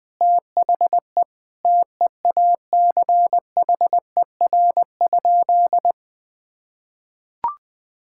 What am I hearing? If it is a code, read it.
Morse code, heard as a keyed tone, about 20 wpm, 700 Hz. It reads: THE TEACHER?
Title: Sentences from Top 1000 Words - Review Part 6 - 20wpm